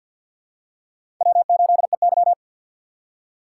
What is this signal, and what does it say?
Morse code, heard as a keyed tone, about 47 wpm, 690 Hz. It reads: W8EY
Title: USA Call Signs - 4 Characters - 6x Character Spacing - 50wpm